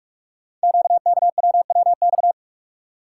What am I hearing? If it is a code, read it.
Morse code, heard as a keyed tone, about 44 wpm, 700 Hz. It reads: QKWWX
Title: Code Groups - 5 Letters - 45wpm